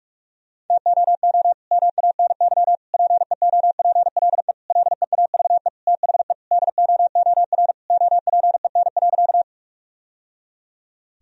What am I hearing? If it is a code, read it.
Morse code, heard as a keyed tone, about 46 wpm, 700 Hz. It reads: TOO MANY PEOPLE LEAVE THE DOOR OPEN.
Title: Sentences from Top 1000 Words - Review Part 1 - 45wpm